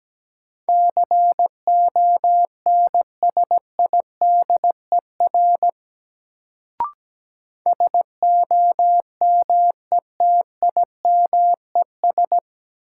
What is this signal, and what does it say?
Morse code, heard as a keyed tone, about 17 wpm, 700 Hz. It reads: CONSIDER SOMETIMES
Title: Sentences from Top 600 Words - New Word Drill - 17wpm